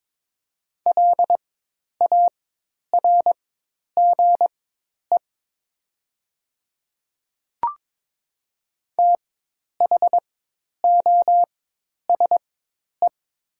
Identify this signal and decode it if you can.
Morse code, heard as a keyed tone, about 22 wpm, spaced as for 10 wpm, 700 Hz. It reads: LARGE THOSE
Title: 4x Character Spacing - Top 300 Words - 22wpm